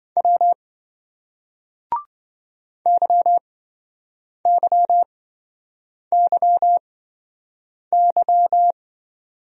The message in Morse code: W YYYY

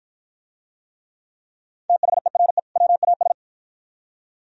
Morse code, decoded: THERE WAS

